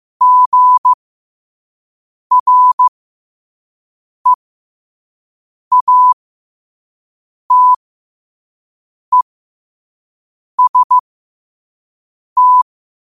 Morse code, heard as a keyed tone, about 15 wpm, spaced as for 5 wpm, 1000 Hz. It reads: GREATEST
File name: qotdfiles / 5wpm